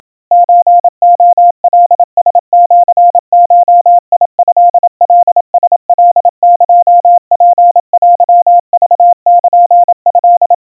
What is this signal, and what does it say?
Morse code (Japanese Wabun code), heard as a keyed tone, about 27 wpm, 700 Hz. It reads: ソレカラシゴトカラカエツテクルト
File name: kawa / Morse sample / kamyu 100 cpm 700 Hz Crispy